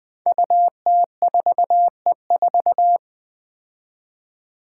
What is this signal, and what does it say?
Morse code, heard as a keyed tone, about 20 wpm, 700 Hz. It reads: UT4E4
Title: Five Challenging Letter-Number Combinations 20wpm